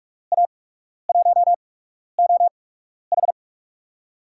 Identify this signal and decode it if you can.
Morse code, heard as a keyed tone, about 43 wpm, spaced as for 12 wpm, 700 Hz. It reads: A1OH
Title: USA Call Signs - 5 Characters - 8x Character Spacing - 45wpm